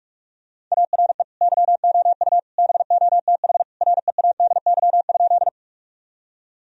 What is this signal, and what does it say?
Morse code, heard as a keyed tone, about 46 wpm, 700 Hz. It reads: ARE YOU BOTH READY?
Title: Sentences from Top 1000 Words - Review Part 1 - 45wpm